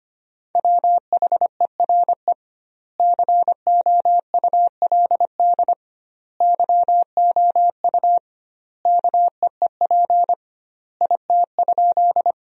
Morse code, heard as a keyed tone, about 25 wpm, 700 Hz. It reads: WHERE COULD YOU KEEP IT?
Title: 2x Word-Spacing - Sentences from Top 300 Words - 25wpm